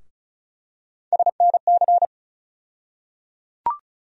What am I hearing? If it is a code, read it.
Morse code, heard as a keyed tone, about 35 wpm, 700 Hz. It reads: SNC
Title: USA Call Sign Postfixes - 3 Letters 35wpm